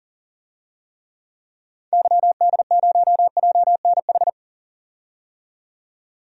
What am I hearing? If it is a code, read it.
Morse code, heard as a keyed tone, about 40 wpm, 700 Hz. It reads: YD0JNH
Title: DXCC-100 Call Signs - 6 Characters - 40wpm